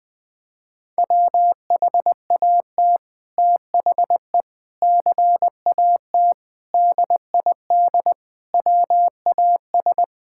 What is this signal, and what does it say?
Morse code, heard as a keyed tone, about 20 wpm, 700 Hz. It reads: WHAT THE CAT DID WAS
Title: Sentences with 2-Word Phrases - Spoken Beginning - 20wpm